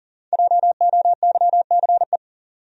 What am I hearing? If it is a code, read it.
Morse code, heard as a keyed tone, about 38 wpm, 700 Hz. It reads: JOYCE
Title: USA Common Names - 40wpm